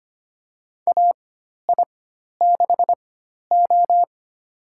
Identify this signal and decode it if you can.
Morse code, heard as a keyed tone, about 25 wpm, spaced as for 12 wpm, 700 Hz. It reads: AI6O
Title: USA Call Signs - 4 Characters - 4x Character Spacing - 25wpm